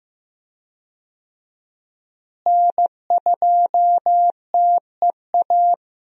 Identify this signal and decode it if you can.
Morse code, heard as a keyed tone, about 15 wpm, 700 Hz. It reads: N2TEA
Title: USA Call Signs - 5 Characters - Repeated 2x - 15wpm